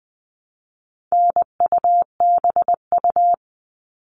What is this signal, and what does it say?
Morse code, heard as a keyed tone, about 20 wpm, 700 Hz. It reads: NUBU
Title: All Cut Numbers - 4-Digit 20wpm